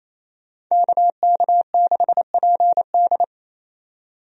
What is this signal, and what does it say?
Morse code, heard as a keyed tone, about 28 wpm, 700 Hz. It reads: KK6PD
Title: USA Call Signs - 5 Characters - 28wpm